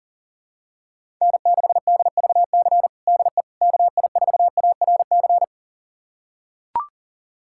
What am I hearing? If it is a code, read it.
Morse code, heard as a keyed tone, about 37 wpm, 700 Hz. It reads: N6DVC DE KI4ARC
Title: USA Callsign DE Callsign 40wpm